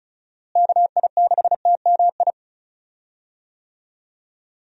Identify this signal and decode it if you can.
Morse code, heard as a keyed tone, about 34 wpm, 700 Hz. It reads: KI6TMI
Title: USA Call Signs - 6 Characters - Repeated 3x - 35wpm